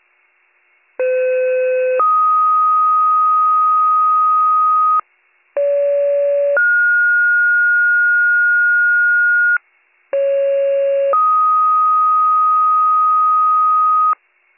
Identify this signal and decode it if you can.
Two-tone page, heard as a one-second tone, then a three-second tone, about 510 then 1250 Hz, about 570 then 1430 Hz, about 550 then 1250 Hz. It is heard from 1.0 to 14.1 s.